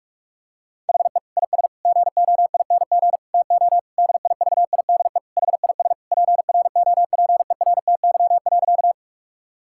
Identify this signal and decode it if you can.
Morse code, heard as a keyed tone, about 42 wpm, 700 Hz. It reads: HE IS GOING TO DIVIDE HIS PROPERTY.
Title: Sentences from Top 1600 Words 45wpm